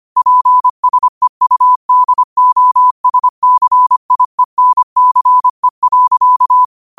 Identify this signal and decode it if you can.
Morse code, heard as a keyed tone, about 25 wpm, 1000 Hz. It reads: PSEUDOSCIENCE.